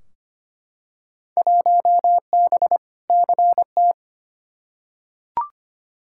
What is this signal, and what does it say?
Morse code, heard as a keyed tone, about 25 wpm, 700 Hz. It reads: 1B CT
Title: Field Day Exchanges Encore - 25wpm